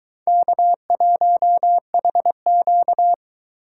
Morse code, heard as a keyed tone, about 20 wpm, 700 Hz. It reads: K1HQ